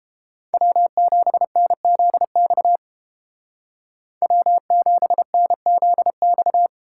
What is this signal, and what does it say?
Morse code, heard as a keyed tone, about 30 wpm, 700 Hz. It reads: W7NZX W7NZX